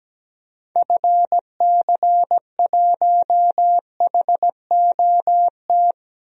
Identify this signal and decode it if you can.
Morse code, heard as a keyed tone, about 17 wpm, 700 Hz. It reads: FC1HOT
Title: DXCC-100 Call Signs - 6 Characters - 17wpm